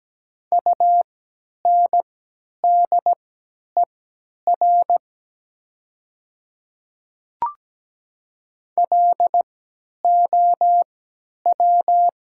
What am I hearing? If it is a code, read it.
Morse code, heard as a keyed tone, about 17 wpm, spaced as for 10 wpm, 700 Hz. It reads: UNDER LOW
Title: 3x Character Spacing - Top 200 Words - 17wpm